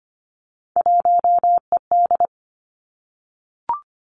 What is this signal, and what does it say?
Morse code, heard as a keyed tone, about 25 wpm, 700 Hz. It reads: 1ED
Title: Three Challenging Letter-Number Combinations 25wpm